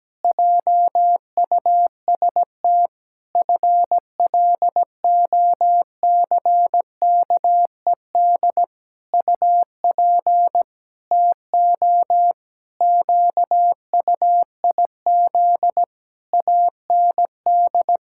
Morse code, heard as a keyed tone, about 17 wpm, 700 Hz. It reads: JUST FLOCKED UP TO QUIZ AND
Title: Pangram Challenge v2 - 17wpm